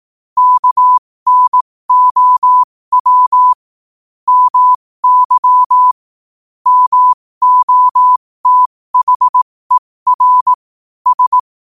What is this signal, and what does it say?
Morse code, heard as a keyed tone, about 18 wpm, spaced as for 15 wpm, 1000 Hz. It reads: KNOW MY MOTHER S
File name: qotdfiles / 15wpm